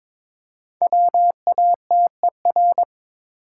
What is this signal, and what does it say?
Morse code, heard as a keyed tone, about 22 wpm, 690 Hz. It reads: WATER
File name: 5 Letter Words - First 3 Letters Given - 22wpm